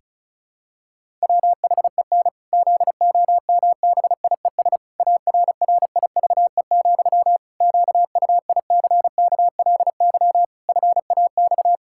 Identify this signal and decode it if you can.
Morse code, heard as a keyed tone, about 34 wpm, 700 Hz. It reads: WHEN ZOMBIES ARRIVE, QUICKLY FAX
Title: Pangram Challenge v2 - 35wpm